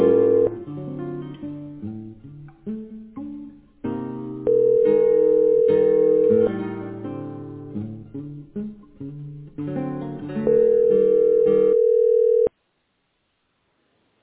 Ringback tone (North America), 440 + 480 Hz.